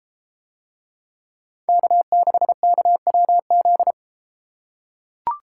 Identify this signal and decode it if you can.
Morse code, heard as a keyed tone, about 30 wpm, 700 Hz. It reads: K6KWZ